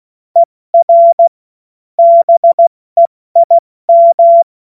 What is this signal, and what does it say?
Morse code, heard as a keyed tone, about 16 wpm, 680 Hz. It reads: ER BEIM
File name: Morsekurs / Witze / Witz 04. Juli 2023 70 F80